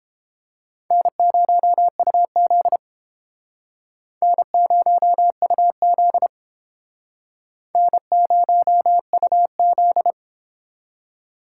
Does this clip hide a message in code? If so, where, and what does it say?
Yes: Morse code from 0.9 to 10.1 s, heard as a keyed tone, about 31 wpm, 700 Hz. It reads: N0UZ N0UZ N0UZ